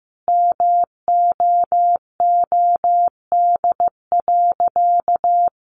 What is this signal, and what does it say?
Morse code, heard as a keyed tone, about 15 wpm, 700 Hz. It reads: MOOD.
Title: Sentences from Top 1100 Words 15wpm